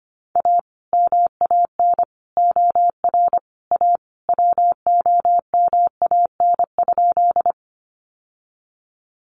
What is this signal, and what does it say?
Morse code, heard as a keyed tone, about 25 wpm, 700 Hz. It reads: A MAN OR A WOMAN?
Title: Sentences from Top 1100 Words 25wpm